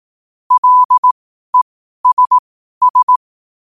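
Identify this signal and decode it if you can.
Morse code, heard as a keyed tone, about 18 wpm, spaced as for 12 wpm, 1000 Hz. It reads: LESS
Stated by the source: Morse sent at 12 wpm.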